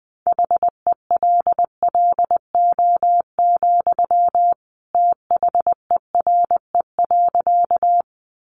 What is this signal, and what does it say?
Morse code, heard as a keyed tone, about 20 wpm, 700 Hz. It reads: HELLO, THERE.
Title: Sentences from Top 1100 Words 20wpm